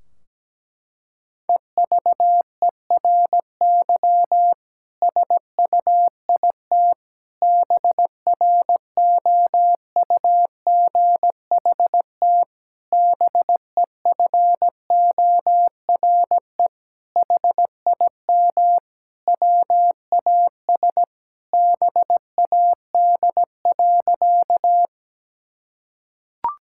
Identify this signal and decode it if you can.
Morse code, heard as a keyed tone, about 17 wpm, 700 Hz. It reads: EVERY SUIT BROUGHT BEFORE HIM WAS BAD.
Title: Sentences from Top 1600 Words 17wpm